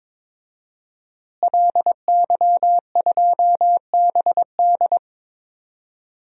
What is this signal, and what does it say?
Morse code, heard as a keyed tone, about 22 wpm, 700 Hz. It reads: LY2BD
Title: DXCC-100 Call Signs - 5 Characters - Repeated 3x - 22wpm